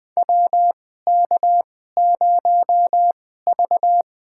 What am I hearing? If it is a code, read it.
Morse code, heard as a keyed tone, about 20 wpm, spaced as for 15 wpm, 700 Hz. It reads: WK0V
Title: USA Call Signs - 2x Character Spacing - 20wpm